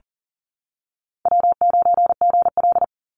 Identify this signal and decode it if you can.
Morse code, heard as a keyed tone, about 40 wpm, 700 Hz. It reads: W9GL
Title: Easy Call Signs - 40wpm